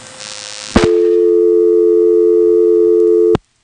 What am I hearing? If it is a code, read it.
Dial tone (North America), 350 + 440 Hz.